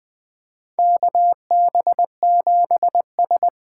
Morse code, heard as a keyed tone, about 20 wpm, 700 Hz. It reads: KB7S
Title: USA Call Signs - 6 Characters - 8x Character Spacing - 20wpm